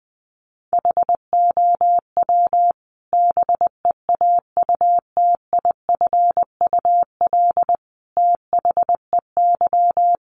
Morse code, heard as a keyed tone, about 20 wpm, 700 Hz. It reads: HOW BEAUTIFUL THEY
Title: Sentences from Top 900 Words 20wpm